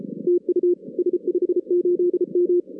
Morse code, heard as a keyed tone, about 33 wpm, 380 Hz. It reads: TU S58M
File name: 2016 arrl dx cw / 2016-ARRL-DX-CW-K5TR- - 0-41